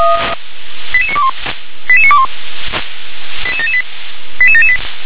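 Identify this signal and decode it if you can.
Five-tone selective call (ZVEI), heard as quick runs of five tones, tones of about 70 ms. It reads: 88921 88921 88778 88778
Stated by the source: ZVEI five-tone selective call received on an SDR.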